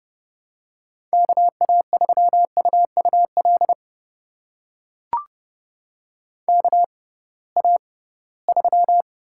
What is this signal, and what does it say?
Morse code, heard as a keyed tone, about 32 wpm, spaced as for 20 wpm, 700 Hz. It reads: KA3UUL KA3
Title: USA Call Signs - 6 Characters - 6x Character Spacing - 30wpm